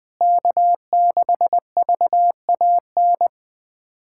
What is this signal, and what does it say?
Morse code, heard as a keyed tone, about 20 wpm, 700 Hz. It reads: K6VAN
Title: SOTA QSO Practice 20wpm